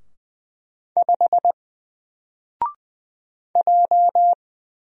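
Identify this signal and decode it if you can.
Morse code, heard as a keyed tone, about 20 wpm, 700 Hz. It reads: 5 J
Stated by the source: Single letter and number Morse drill at 20 wpm, from N4LCD.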